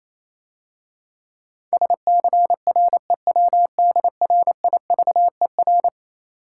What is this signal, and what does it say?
Morse code, heard as a keyed tone, about 28 wpm, 700 Hz. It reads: SCREWDRIVER
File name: Easy Words - Speed-Racing with Hints - 22wpm